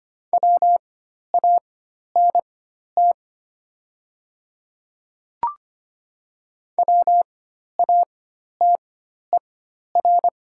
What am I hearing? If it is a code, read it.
Morse code, heard as a keyed tone, about 25 wpm, spaced as for 12 wpm, 700 Hz. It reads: WANT WATER
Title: 4x Character Spacing - Top 200 Words - 25wpm